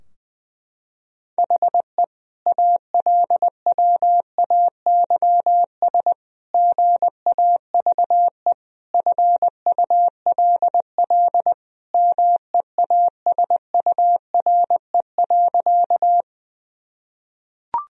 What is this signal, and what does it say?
Morse code, heard as a keyed tone, about 20 wpm, 700 Hz. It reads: HE ALWAYS GAVE FULL MEASURE.
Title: Sentences from Top 1000 Words - Review Part 3 - 20wpm